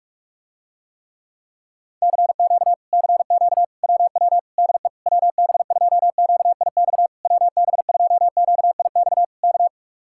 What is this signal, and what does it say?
Morse code, heard as a keyed tone, about 49 wpm, 700 Hz. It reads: CQ CQ WW DE WB2QIX WB2QIX K